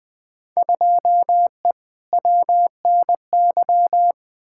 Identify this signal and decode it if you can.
Morse code, heard as a keyed tone, about 20 wpm, 700 Hz. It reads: 2E WNY
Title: Field Day Exchanges Encore - 20wpm